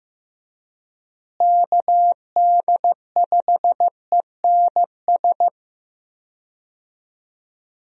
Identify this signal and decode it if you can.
Morse code, heard as a keyed tone, about 16 wpm, 700 Hz. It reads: KD5ENS
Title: USA Call Signs - 6 Characters - Repeated 2x - 15wpm